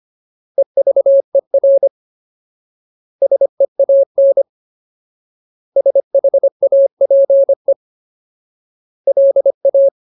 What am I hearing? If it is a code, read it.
Morse code, heard as a keyed tone, about 25 wpm, 550 Hz. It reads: EVER SEAN SHAPE LA
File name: Ultimate Word List / 25WPM Ultimate